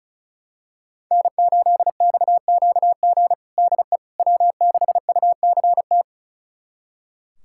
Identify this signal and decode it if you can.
Morse code, heard as a keyed tone, about 34 wpm, 700 Hz. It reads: N8XQG DE W6UCT
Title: USA Callsign DE Callsign - 5 Characters - 35wpm